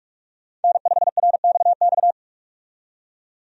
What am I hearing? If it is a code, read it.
Morse code, heard as a keyed tone, about 45 wpm, 700 Hz. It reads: N5RXX
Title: USA Call Signs - 5 Characters - 45wpm